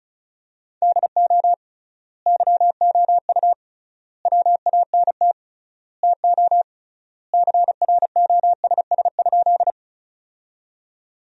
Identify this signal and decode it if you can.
Morse code, heard as a keyed tone, about 36 wpm, 700 Hz. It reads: DO YOU WANT TO CROSS?
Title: 3x Word-Spacing - Sentences from Top 300 Words - 35wpm